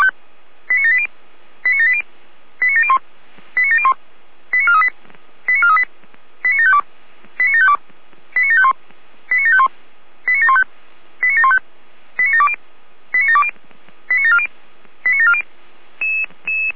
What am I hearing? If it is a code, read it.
Five-tone selective call (ZVEI), heard as quick runs of five tones, tones of about 80 ms. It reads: 89780 89780 89701 89701 89438 89438 89752 89752 89751 89751 89715 89715 89720 89720 89740 89740